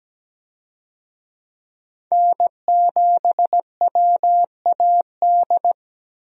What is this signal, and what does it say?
Morse code, heard as a keyed tone, about 17 wpm, 700 Hz. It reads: N7WAD
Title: USA Call Signs - 5 Characters - Repeated 2x - 17wpm